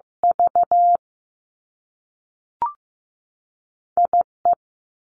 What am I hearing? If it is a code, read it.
Morse code, heard as a keyed tone, about 15 wpm, 700 Hz. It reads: V I E
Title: Two Challenging Letter-Number Combinations 15wpm